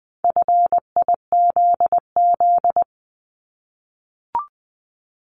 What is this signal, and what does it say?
Morse code, heard as a keyed tone, about 20 wpm, 700 Hz. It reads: FIZZ
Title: Multi-letter Phonemes Repeated 3x 20wpm